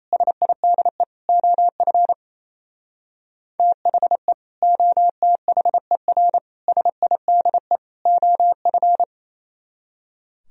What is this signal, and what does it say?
Morse code, heard as a keyed tone, about 36 wpm, 690 Hz. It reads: SIDE OF THE OTHER SIDE OF